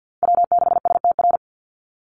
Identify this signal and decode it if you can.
Morse code, heard as a keyed tone, about 49 wpm, 700 Hz. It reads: W6STR
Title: NA SOTA Chaser Callsigns 50wpm